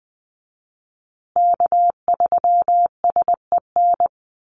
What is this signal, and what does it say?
Morse code, heard as a keyed tone, about 20 wpm, 700 Hz. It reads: K3SEN
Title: Call Signs Repeated 2x - 20wpm